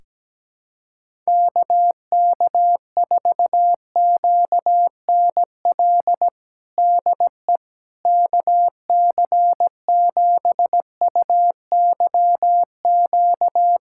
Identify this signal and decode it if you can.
Morse code, heard as a keyed tone, about 17 wpm, 700 Hz. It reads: KK4QNL DE KC7UYQ
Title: USA Callsign DE Callsign - 6 Characters - 17wpm